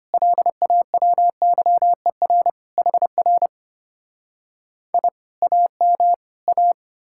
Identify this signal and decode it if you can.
Morse code, heard as a keyed tone, about 33 wpm, 700 Hz. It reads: LAWYER HR I AM A